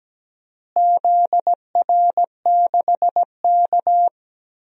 Morse code, heard as a keyed tone, about 17 wpm, 700 Hz. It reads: ZR6K
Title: DXCC-100 Call Signs - 4 Characters - Repeated 3x - 17wpm